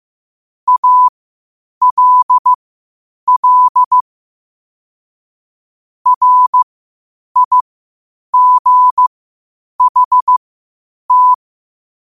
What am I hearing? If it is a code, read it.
Morse code, heard as a keyed tone, about 15 wpm, spaced as for 8 wpm, 1000 Hz. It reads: ALL RIGHT
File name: 8wpm